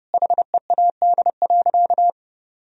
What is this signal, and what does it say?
Morse code, heard as a keyed tone, about 32 wpm, 700 Hz. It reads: HEAD.
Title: Sentences from Top 300 Words and Abbreviations 30wpm